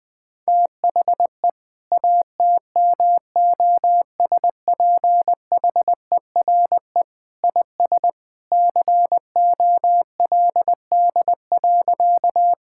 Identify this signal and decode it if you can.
Morse code, heard as a keyed tone, about 20 wpm, 700 Hz. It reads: THE ATMOSPHERE IS COLD.
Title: Sentences from Top 1600 Words 20wpm